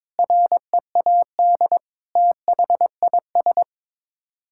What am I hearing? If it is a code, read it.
Morse code, heard as a keyed tone, about 22 wpm, 700 Hz. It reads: READ THIS
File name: Sets of 9 Words - 22wpm